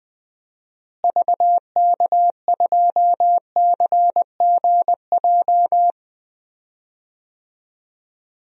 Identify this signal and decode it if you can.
Morse code, heard as a keyed tone, about 21 wpm, 700 Hz. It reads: VK2CGJ
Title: DXCC-100 Call Signs - 6 Characters - Repeated 3x - 20wpm